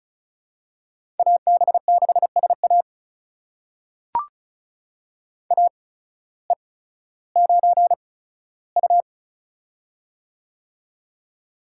Morse code, heard as a keyed tone, about 36 wpm, spaced as for 16 wpm, 700 Hz. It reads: AB6SA AE9U
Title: USA Call Signs - 8x Character Spacing - 35wpm